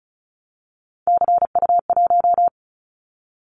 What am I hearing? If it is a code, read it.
Morse code, heard as a keyed tone, about 36 wpm, 700 Hz. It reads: CU1
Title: Callsign Prefixes for Top 100 DXCC - 35wpm